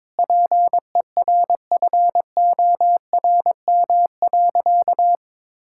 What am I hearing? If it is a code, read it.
Morse code, heard as a keyed tone, about 22 wpm, 700 Hz. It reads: PERFORM.
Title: Sentences from Top 1000 Words - Review Part 5 - 22wpm